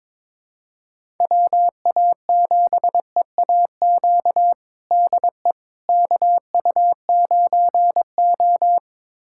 Morse code, heard as a keyed tone, about 22 wpm, 700 Hz. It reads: WA7EAQ DE KU9O
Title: USA Callsign DE Callsign 22wpm